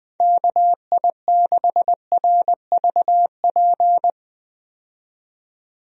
Morse code, heard as a keyed tone, about 20 wpm, 700 Hz. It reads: KI6RVP
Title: USA Call Signs - 6 Characters - 20wpm